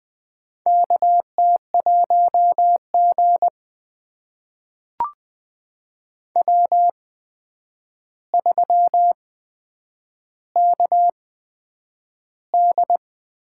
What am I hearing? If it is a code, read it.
Morse code, heard as a keyed tone, about 20 wpm, spaced as for 9 wpm, 700 Hz. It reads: KT1G W3KD
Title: USA Call Signs - 4 Characters - 8x Character Spacing - 20wpm